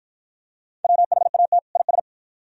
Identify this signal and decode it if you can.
Morse code, heard as a keyed tone, about 50 wpm, 700 Hz. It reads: WHAT IS